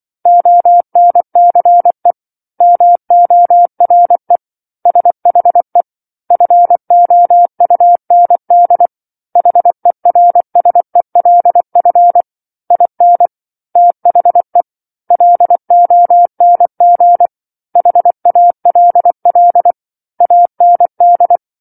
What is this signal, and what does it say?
Morse code, heard as a keyed tone, about 24 wpm, 700 Hz. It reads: ONCE MORE SHE FOUND HERSELF IN THE LONG HALL AND
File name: 07-Chapter-07 24 WPM